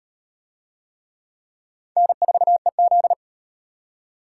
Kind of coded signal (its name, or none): Morse code